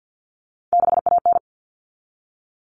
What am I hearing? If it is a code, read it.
Morse code, heard as a keyed tone, about 50 wpm, 700 Hz. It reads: 6AN
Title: Common Cut Numbers - 3-Digit 50wpm